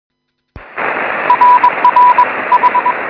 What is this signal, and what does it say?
Morse code, heard as a keyed tone, about 21 wpm, 1000 Hz, with radio noise. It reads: RRH
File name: sm7lcb / 09-sm0nue 23cm